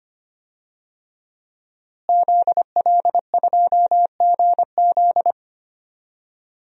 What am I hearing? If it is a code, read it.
Morse code, heard as a keyed tone, about 25 wpm, 700 Hz. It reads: ZL2GZ
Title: DXCC-100 Call Signs - 5 Characters - Repeated 3x - 25wpm